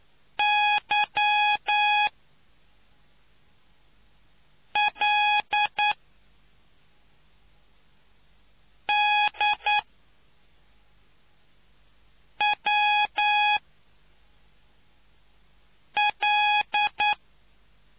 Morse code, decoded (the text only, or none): Y L D W L